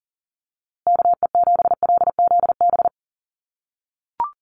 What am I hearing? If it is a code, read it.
Morse code, heard as a keyed tone, about 40 wpm, 700 Hz. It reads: KE7LZB
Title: USA Call Signs Repeated 2x 40wpm